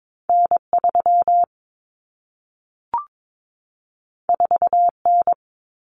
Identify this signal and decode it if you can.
Morse code, heard as a keyed tone, about 22 wpm, 700 Hz. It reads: N3 4N
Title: Common Cut Numbers - 2-Digit 22wpm